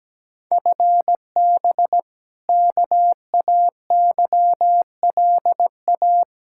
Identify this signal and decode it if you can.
Morse code, heard as a keyed tone, about 17 wpm, 700 Hz. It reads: FB KAYLA
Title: K1USN SST QSO Practice 17wpm